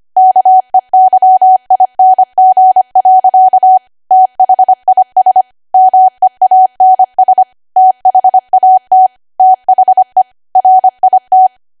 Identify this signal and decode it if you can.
Morse code, heard as a keyed tone, about 25 wpm, 750 Hz. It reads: KEYING. THIS MEANS THAT THE RIT
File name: Archive / 25 WPM / 25WPM